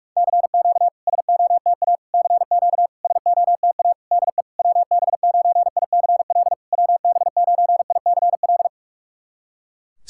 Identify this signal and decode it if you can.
Morse code, heard as a keyed tone, about 46 wpm, 700 Hz. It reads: CQ SOTA CQ SOTA DE WB9ICL WB9ICL
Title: SOTA QSO Practice 45wpm